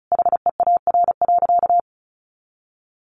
Morse code, heard as a keyed tone, about 34 wpm, 700 Hz. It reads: HEAR.
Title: A Sound Of Thunder 35wpm